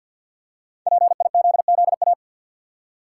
Morse code, heard as a keyed tone, about 52 wpm, 700 Hz. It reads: PIZZA
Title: Five Letter Words - 50wpm